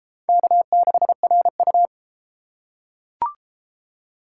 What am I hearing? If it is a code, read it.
Morse code, heard as a keyed tone, about 31 wpm, 700 Hz. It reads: K6RU